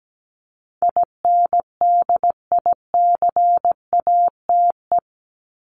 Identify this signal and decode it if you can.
Morse code, heard as a keyed tone, about 17 wpm, 700 Hz. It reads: INDICATE
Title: ED Suffix - Top 100 Words with Base Words 17wpm